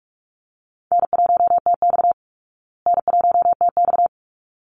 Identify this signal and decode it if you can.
Morse code, heard as a keyed tone, about 44 wpm, 700 Hz. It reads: N1TX N1TX
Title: Call Signs Repeated 3x - 45wpm